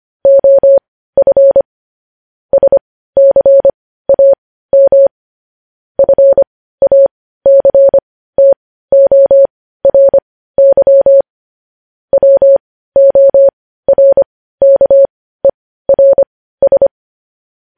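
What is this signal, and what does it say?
Morse code, heard as a keyed tone, about 25 wpm, spaced as for 15 wpm, 550 Hz. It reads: OF SCAM FACTORY WORKERS
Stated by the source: Morse code practice of news headlines at 15 wpm, opening with vvv.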